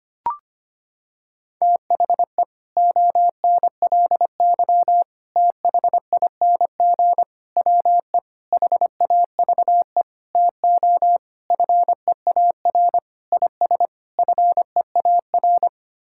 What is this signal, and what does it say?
Morse code, heard as a keyed tone, about 25 wpm, 700 Hz. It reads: THE ONLY THING WE HAVE TO FEAR IS FEAR
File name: Sets of 9 Words - 25wpm